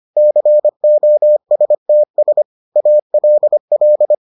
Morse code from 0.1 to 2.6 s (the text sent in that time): COSTS